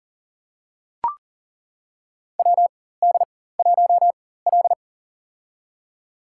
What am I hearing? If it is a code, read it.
Morse code, heard as a keyed tone, about 37 wpm, spaced as for 18 wpm, 700 Hz. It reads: WD1L
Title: USA Call Signs - 4 Characters - 4x Character Spacing - 40wpm